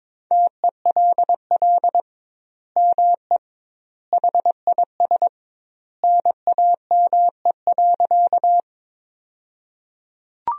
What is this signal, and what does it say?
Morse code, heard as a keyed tone, about 22 wpm, 700 Hz. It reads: TELL ME HIS NAME.
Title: 2x Word-Spacing - Sentences from Top 200 Words - 22wpm